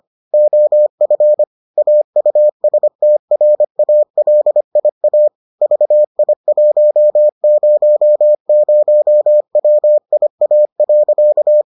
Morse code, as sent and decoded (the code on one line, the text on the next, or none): --- ..-. / .- ..- ... - .-. .- .-.. .. .- / ...- .. .---- ----- ----- .-- .. .- .-.-.-
OF AUSTRALIA VI100WIA.